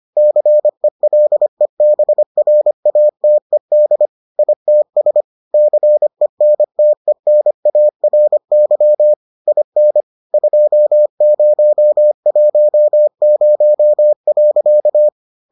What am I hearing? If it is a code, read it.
Morse code, heard as a keyed tone, about 25 wpm, 600 Hz. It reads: CELEBRATED ITS CENTENARY IN 2010.